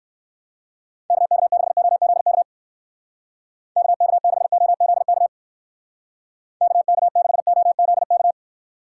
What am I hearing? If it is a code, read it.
Morse code, heard as a keyed tone, about 55 wpm, 700 Hz. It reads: KK6OZK KK6OZK KK6OZK
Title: Speed Racing - USA 2x3 Call Signs 45wpm